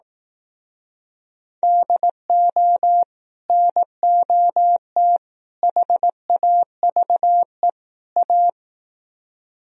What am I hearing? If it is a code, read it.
Morse code, heard as a keyed tone, about 19 wpm, 700 Hz. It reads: DO NOT HAVE A
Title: Speed-Racing - Sets of 4 Words 15wpm